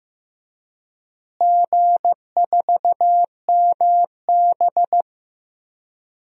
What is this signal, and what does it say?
Morse code, heard as a keyed tone, about 15 wpm, 700 Hz. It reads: G4MB
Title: DXCC-100 Call Signs - 4 Characters - Repeated 3x - 15wpm